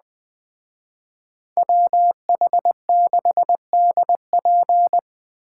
Morse code, heard as a keyed tone, about 20 wpm, 700 Hz. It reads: WH6DP